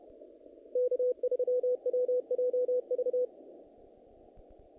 Morse code, heard as a keyed tone, about 33 wpm, 500 Hz. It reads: K3WJV
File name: contests / naqp-cw-k5tr-op-k5pi- -2000